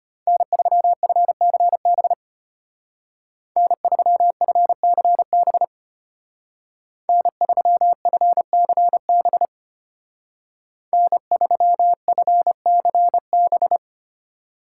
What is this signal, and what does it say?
Morse code, heard as a keyed tone, about 33 wpm, 700 Hz. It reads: N3FCB N3FCB N3FCB N3FCB